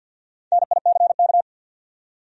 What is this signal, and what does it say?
Morse code, heard as a keyed tone, about 48 wpm, 700 Hz. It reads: NICK
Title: USA Common Names - Max 4 Letters - 50wpm